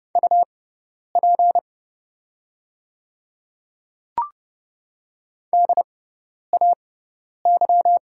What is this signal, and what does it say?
Morse code, heard as a keyed tone, about 30 wpm, spaced as for 10 wpm, 690 Hz. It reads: UP DAY
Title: 6x Character Spacing - Top 200 Words - 30wpm